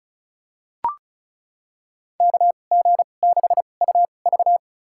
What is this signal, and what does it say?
Morse code, heard as a keyed tone, about 36 wpm, spaced as for 25 wpm, 700 Hz. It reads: KG6UV
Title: USA Call Signs - 5 Characters - 2x Character Spacing - 35wpm